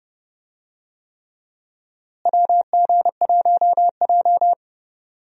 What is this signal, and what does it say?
Morse code, heard as a keyed tone, about 30 wpm, 700 Hz. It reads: WG1J
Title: USA Call Signs - 4 Characters - Repeated 3x - 30wpm